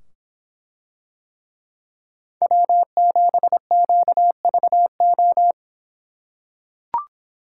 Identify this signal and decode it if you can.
Morse code, heard as a keyed tone, about 25 wpm, 700 Hz. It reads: W7QVO